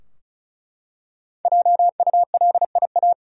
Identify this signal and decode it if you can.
Morse code, heard as a keyed tone, about 35 wpm, 700 Hz. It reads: JULIA
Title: USA Common Names - Max 5 Letters - 35wpm